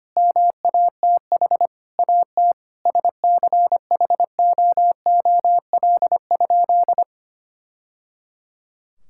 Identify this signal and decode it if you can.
Morse code, heard as a keyed tone, about 25 wpm, 700 Hz. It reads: MATH AT SCHOOL?
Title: Sentences from Top 1000 Words - Review Part 6 - 25wpm